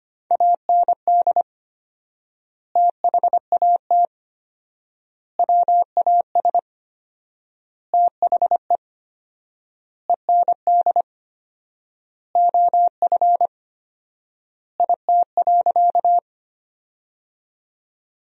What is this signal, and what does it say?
Morse code, heard as a keyed tone, about 25 wpm, 700 Hz. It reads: AND THAT WAS THE END OF IT.
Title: 4x Word-Spacing - Sentences from Top 200 Words - 25wpm